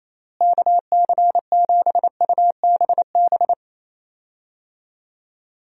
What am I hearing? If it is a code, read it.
Morse code, heard as a keyed tone, about 28 wpm, 700 Hz. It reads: KC7UBB
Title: USA Call Signs - 6 Characters - Repeated 2x - 28wpm